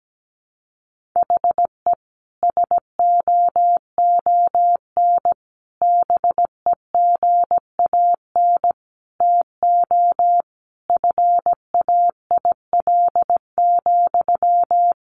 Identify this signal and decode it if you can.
Morse code, heard as a keyed tone, about 17 wpm, 700 Hz. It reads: HE SOON BEGAN TO FAIL,
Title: Sentences from Top 1100 Words 17wpm